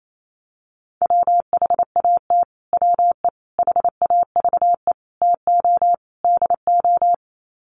Morse code, heard as a keyed tone, about 28 wpm, 700 Hz. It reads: WHAT WE HAVE TO DO
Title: Sets of 5 Words Encore 28wpm